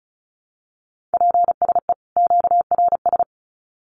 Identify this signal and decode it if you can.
Morse code, heard as a keyed tone, about 34 wpm, 700 Hz. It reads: PSE QRS
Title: Common QSO Phrases 35wpm